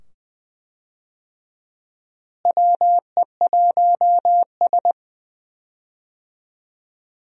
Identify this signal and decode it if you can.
Morse code, heard as a keyed tone, about 21 wpm, 700 Hz. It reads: WE1S
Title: USA Call Signs - 4 Characters - Repeated 2x - 20wpm